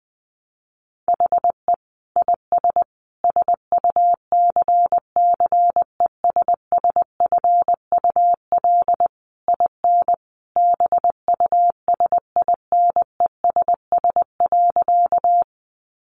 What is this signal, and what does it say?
Morse code, heard as a keyed tone, about 19 wpm, 700 Hz. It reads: HE IS SUCCESSFUL IN BUSINESS.
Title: Sentences from Top 900 Words 20wpm